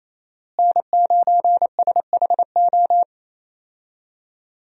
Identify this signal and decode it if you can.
Morse code, heard as a keyed tone, about 28 wpm, 700 Hz. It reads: N9SHO
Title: USA Call Signs - 5 Characters - Repeated 2x - 28wpm